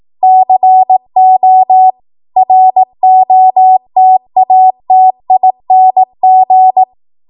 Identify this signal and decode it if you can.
Morse code, heard as a keyed tone, about 18 wpm, 750 Hz. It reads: CO ROTATING
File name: Morse / 18WPM